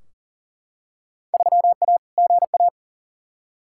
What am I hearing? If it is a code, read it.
Morse code, heard as a keyed tone, about 39 wpm, 700 Hz. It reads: 3A GA
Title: Field Day Exchanges - 40wpm